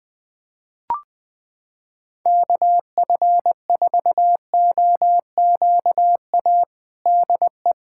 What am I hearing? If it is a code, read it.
Morse code, heard as a keyed tone, about 20 wpm, 700 Hz. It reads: KF4OQA DE
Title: USA Callsign DE Callsign - 6 Characters - 20wpm